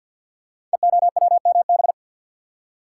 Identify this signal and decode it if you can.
Morse code, heard as a keyed tone, about 47 wpm, 700 Hz. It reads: EOWMB
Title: Code Groups - 5 Letter:Numbers - 50wpm